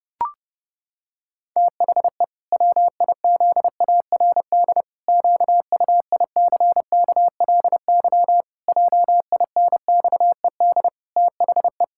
Morse code, heard as a keyed tone, about 30 wpm, 700 Hz. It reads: THE WIZARD QUICKLY JINXED THE
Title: Pangram Challenge v2 - 30wpm